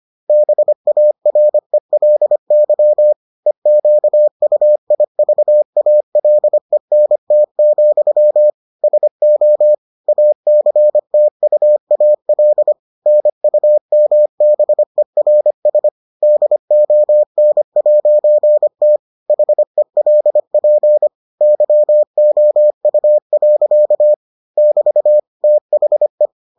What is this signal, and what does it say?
Morse code, heard as a keyed tone, about 25 wpm, 600 Hz. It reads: BARELY EQUIVALENT, SO ACTUAL NUMBERS DON'T HELP YOU. <BT> THE